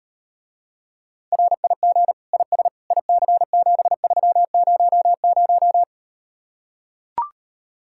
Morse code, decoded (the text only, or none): RIG IS IC7300